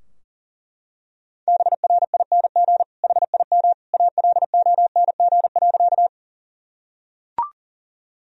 Morse code, heard as a keyed tone, about 40 wpm, 700 Hz. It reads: BRING HIM ALONG.
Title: Sentences from Top 1000 Words - Review Part 3 - 40wpm